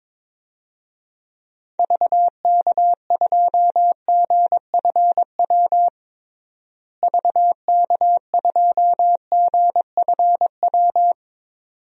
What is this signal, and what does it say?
Morse code, heard as a keyed tone, about 22 wpm, 700 Hz. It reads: VK2GFW VK2GFW
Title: DXCC-100 Call Signs - 6 Characters - Repeated 2x - 22wpm